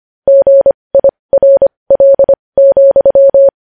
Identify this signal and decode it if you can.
Morse code, heard as a keyed tone, about 25 wpm, spaced as for 20 wpm, 550 Hz. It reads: GIRL,
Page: news-headlines-in-morse-code-at-20-wpm